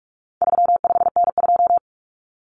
Morse code, heard as a keyed tone, about 44 wpm, 700 Hz. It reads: 35N2
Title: Common Cut Numbers - 4-Digit 45wpm